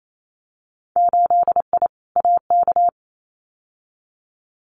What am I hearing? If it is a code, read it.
Morse code, heard as a keyed tone, about 28 wpm, 700 Hz. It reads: 8I AK
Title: Winter Field Day Exchanges Encore 28wpm